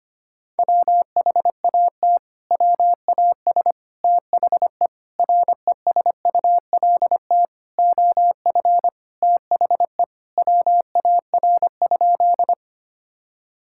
Morse code, heard as a keyed tone, about 25 wpm, 700 Hz. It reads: WHAT WAS THE RESULT OF THE WAR?
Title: Sentences from Top 1000 Words - Review Part 5 - 25wpm